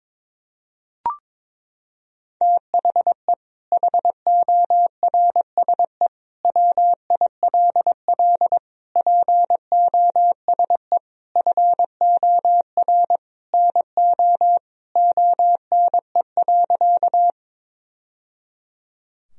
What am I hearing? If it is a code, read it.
Morse code, heard as a keyed tone, about 21 wpm, 700 Hz. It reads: THE HORSE WILL POSE FOR NO ONE.
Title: Sentences from Top 1000 Words - Review Part 2 - 22wpm